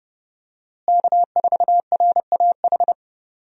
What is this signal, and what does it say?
Morse code, heard as a keyed tone, about 30 wpm, 700 Hz. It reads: K4RAH
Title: USA Call Signs - 5 Characters - Repeated 3x - 30wpm